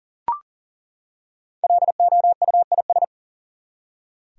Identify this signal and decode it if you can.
Morse code, heard as a keyed tone, about 41 wpm, 700 Hz. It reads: LOUIS
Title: USA Common Names - Max 5 Letters - 40wpm